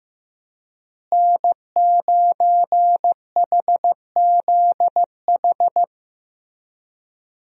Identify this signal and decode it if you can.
Morse code, heard as a keyed tone, about 16 wpm, 700 Hz. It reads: N9HZH
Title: USA Call Signs - 5 Characters - Repeated 3x - 15wpm